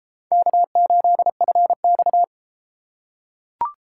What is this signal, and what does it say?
Morse code, heard as a keyed tone, about 32 wpm, 700 Hz. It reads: K8FX